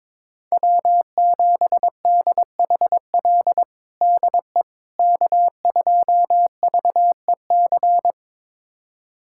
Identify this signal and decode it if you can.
Morse code, heard as a keyed tone, about 22 wpm, 700 Hz. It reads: W7DHL DE K2VEC
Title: USA Callsign DE Callsign - 5 Characters - 22wpm